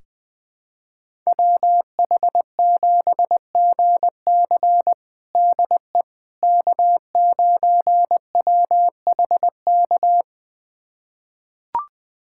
Morse code, heard as a keyed tone, about 20 wpm, 700 Hz. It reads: WH7GC DE K9WHK
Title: USA Callsign DE Callsign - 5 Characters - 20wpm